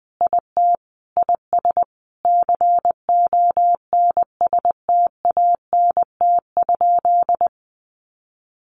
Morse code, heard as a keyed tone, about 20 wpm, 700 Hz. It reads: IT IS CONSTANT?
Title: Sentences from Top 1500 Words 20wpm